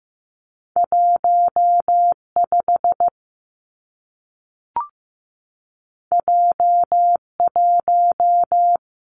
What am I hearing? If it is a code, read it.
Morse code, heard as a keyed tone, about 15 wpm, 700 Hz. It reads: 1 5 J 1
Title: Two Challenging Letter-Number Combinations 15wpm